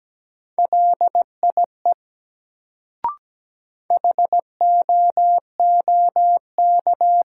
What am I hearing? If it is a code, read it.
Morse code, heard as a keyed tone, about 17 wpm, 700 Hz. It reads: LIE HOOK
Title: Sentences from Top 1600 Words - New Word Drill Alt - 17wpm